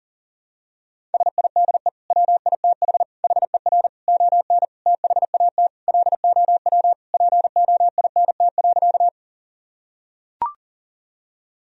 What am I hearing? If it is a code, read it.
Morse code, heard as a keyed tone, about 37 wpm, 700 Hz. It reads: SIDE WITH HER ON THAT LOW POINT.
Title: Sentences from Top 1000 Words - Review Part 1 - 40wpm